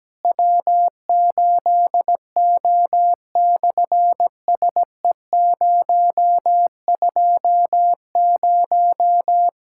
Morse code, heard as a keyed tone, about 17 wpm, 700 Hz. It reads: W8O/SE020
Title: SOTA QSO Practice 17wpm